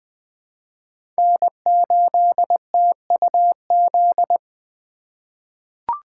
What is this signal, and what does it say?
Morse code, heard as a keyed tone, about 20 wpm, 700 Hz. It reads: N8TUZ
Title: USA Call Signs - 5 Characters - 4x Character Spacing - 20wpm